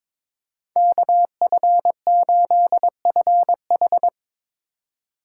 Morse code, heard as a keyed tone, about 27 wpm, 700 Hz. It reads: KF8FH
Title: Speed Racing - USA 2x2 Call Signs 22wpm